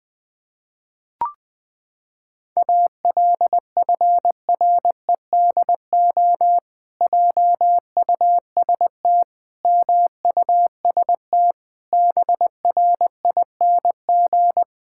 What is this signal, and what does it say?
Morse code, heard as a keyed tone, about 20 wpm, 700 Hz. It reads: ALFREDO JUST MUST BRING
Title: Pangram Challenge v2 - 20wpm